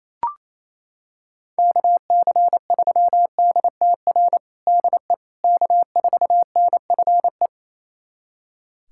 Morse code, heard as a keyed tone, about 28 wpm, 700 Hz. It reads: KC3DTR DE K4NFE
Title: USA Callsign DE Callsign 28wpm